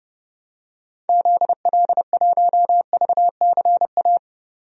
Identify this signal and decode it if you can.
Morse code, heard as a keyed tone, about 30 wpm, 700 Hz. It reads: ZL1VCA
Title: DXCC-100 Call Signs - 6 Characters - 30wpm